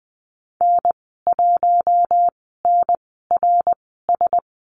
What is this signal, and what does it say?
Morse code, heard as a keyed tone, about 20 wpm, spaced as for 15 wpm, 700 Hz. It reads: N1NRS
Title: 062 Lesson 11g - Letter h - Call Signs 20wpm